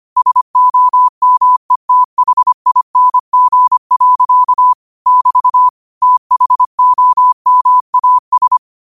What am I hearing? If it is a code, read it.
Morse code, heard as a keyed tone, about 25 wpm, 1000 Hz. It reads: SOMETHING. <BT> THOMAS
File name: qotdfiles / 25wpm